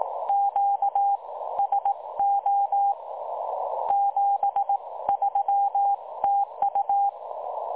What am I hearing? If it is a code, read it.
Morse code, heard as a keyed tone, about 18 wpm, spaced as for 15 wpm, 810 Hz. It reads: QSO 73 TU
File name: dlazaawansowanych / nagraniazpasma / websdr recording 3565.0kHz